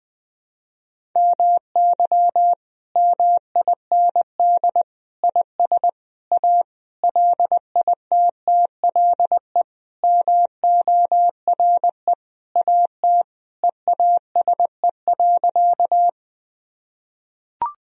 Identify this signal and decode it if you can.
Morse code, heard as a keyed tone, about 20 wpm, 700 Hz. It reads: MY MIND IS A LITTLE MORE AT EASE.
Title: Sentences from Top 1000 Words - Review Part 1 - 20wpm